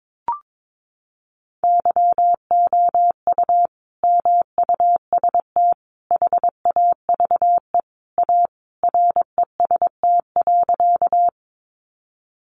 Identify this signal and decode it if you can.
Morse code, heard as a keyed tone, about 22 wpm, 700 Hz. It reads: YOU MUST HAVE A REST.
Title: Sentences from Top 500 Words 22wpm